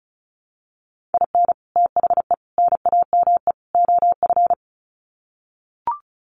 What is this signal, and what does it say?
Morse code, heard as a keyed tone, about 34 wpm, 700 Hz. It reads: IN THE NAME OF
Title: Sets of 4 Words 35wpm